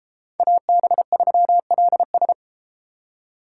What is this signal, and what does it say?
Morse code, heard as a keyed tone, about 31 wpm, 700 Hz. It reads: AB3LS